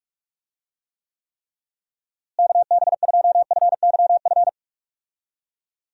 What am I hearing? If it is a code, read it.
Morse code, heard as a keyed tone, about 46 wpm, 700 Hz. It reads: KB2FYF
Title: USA Call Signs - 6 Characters - Repeated 3x - 45wpm